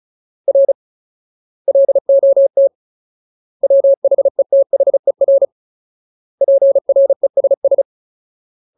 Morse code, heard as a keyed tone, about 34 wpm, 550 Hz. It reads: R LOT WHETHER PRESS
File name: Ultimate Word List / 35WPM Ultimate